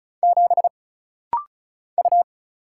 Morse code, heard as a keyed tone, about 36 wpm, 700 Hz. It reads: 7 U
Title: Single Letter-Number - Rapid-Fire 35wpm